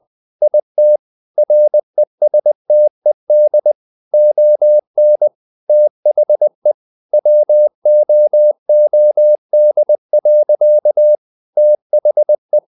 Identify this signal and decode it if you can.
Morse code, heard as a keyed tone, about 20 wpm, 600 Hz. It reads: IT RESTED ON THE WOOD. THE